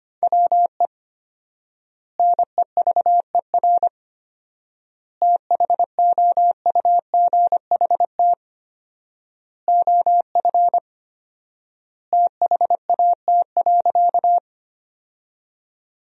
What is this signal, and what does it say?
Morse code, heard as a keyed tone, about 25 wpm, 700 Hz. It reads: WE NEVER THOUGHT OF THAT.
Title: 4x Word-Spacing - Sentences from Top 300 Words - 25wpm